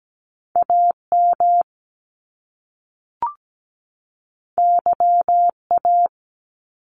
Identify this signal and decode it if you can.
Morse code, heard as a keyed tone, about 17 wpm, 700 Hz. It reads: AM YA
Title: Top Two Letter Combinations in Words 17wpm